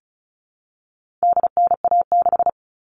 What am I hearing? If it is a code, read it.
Morse code, heard as a keyed tone, about 34 wpm, 700 Hz. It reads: DNA6
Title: All Cut Numbers - 4-Digit 35wpm